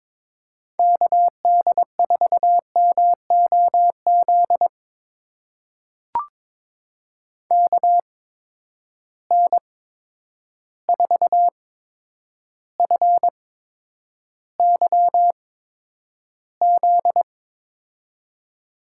Morse code, heard as a keyed tone, about 22 wpm, spaced as for 9 wpm, 700 Hz. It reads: KD4MOZ KN4FYZ
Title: USA Call Signs - 6 Characters - 8x Character Spacing - 22wpm